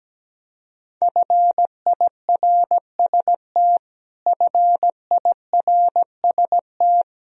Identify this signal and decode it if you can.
Morse code, heard as a keyed tone, about 17 wpm, 700 Hz. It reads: FIRST FIRST
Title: Top 100 Words Repeated 17wpm